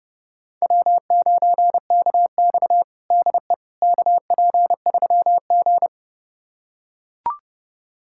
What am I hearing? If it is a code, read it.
Morse code, heard as a keyed tone, about 32 wpm, 700 Hz. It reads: W9KX DE KP3G
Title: USA Callsign DE Callsign - 4 Characters - 30wpm